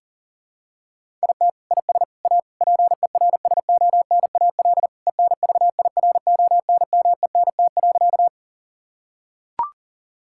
Morse code, heard as a keyed tone, about 40 wpm, 700 Hz. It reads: IT IS A PERSONAL ENVIRONMENT.
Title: Sentences from Top 1000 Words - Review Part 5 - 40wpm